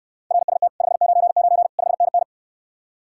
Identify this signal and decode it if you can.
Morse code, heard as a keyed tone, about 69 wpm, 700 Hz. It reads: RST 599 5NN